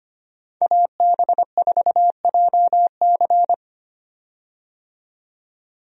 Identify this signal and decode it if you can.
Morse code, heard as a keyed tone, about 25 wpm, 700 Hz. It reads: AB4JC
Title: USA Call Signs - 5 Characters - 25wpm